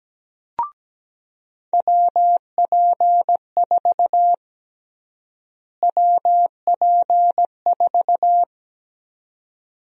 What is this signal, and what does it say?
Morse code, heard as a keyed tone, about 17 wpm, 700 Hz. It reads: WP4 WP4
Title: USA Call Sign Prefixes - Repeated 2x - 17wpm